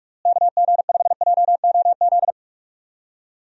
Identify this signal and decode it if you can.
Morse code, heard as a keyed tone, about 46 wpm, 700 Hz. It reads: KG5JOZ
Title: USA Call Signs - 6 Characters - Repeated 2x - 45wpm